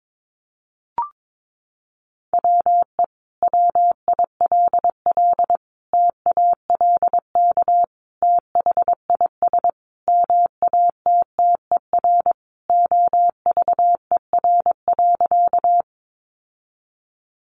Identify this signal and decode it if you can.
Morse code, heard as a keyed tone, about 22 wpm, 700 Hz. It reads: WE WILL TALK THIS MATTER OVER.
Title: Sentences from Top 700 Words 22wpm